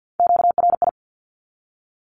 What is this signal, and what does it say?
Morse code, heard as a keyed tone, about 50 wpm, 700 Hz. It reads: QRI
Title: Q-Codes 50wpm